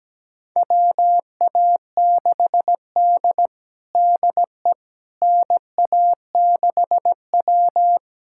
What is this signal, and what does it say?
Morse code, heard as a keyed tone, about 17 wpm, 700 Hz. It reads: WA6D DE NA6W
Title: USA Callsign DE Callsign - 4 Characters - 17wpm